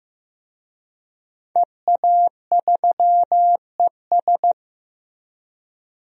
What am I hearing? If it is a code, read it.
Morse code, heard as a keyed tone, about 16 wpm, 700 Hz. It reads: EA3ES
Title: DXCC-100 Call Signs - 5 Characters - Repeated 2x - 15wpm